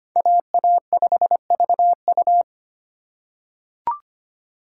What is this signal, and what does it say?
Morse code, heard as a keyed tone, about 25 wpm, 700 Hz. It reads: AA5VU
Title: K1USN SST QSO Practice - 25wpm